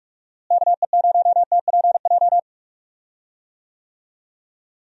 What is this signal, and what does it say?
Morse code, heard as a keyed tone, about 43 wpm, 700 Hz. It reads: KE0TPJ
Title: USA Call Signs - 6 Characters - Repeated 2x - 45wpm